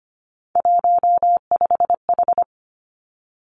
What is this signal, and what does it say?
Morse code, heard as a keyed tone, about 25 wpm, 700 Hz. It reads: 15H
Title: Three Challenging Letter-Number Combinations 25wpm